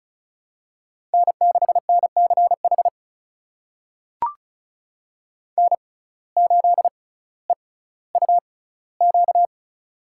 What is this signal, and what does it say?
Morse code, heard as a keyed tone, about 36 wpm, spaced as for 18 wpm, 700 Hz. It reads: N6NCH N8EUQ
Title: USA Call Signs - 5 Characters - 6x Character Spacing - 35wpm